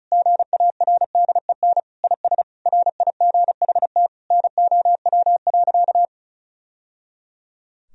Morse code, heard as a keyed tone, about 36 wpm, 700 Hz. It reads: GARDEN IS RIGHT NOW.
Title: Sentences from Top 1000 Words - Review Part 4 - 35wpm